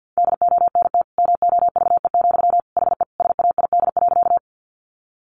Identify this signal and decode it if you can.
Morse code, heard as a keyed tone, about 48 wpm, 700 Hz. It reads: DONT MOVE, HE SAID.
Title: Sentences from Top 300 Words 50wpm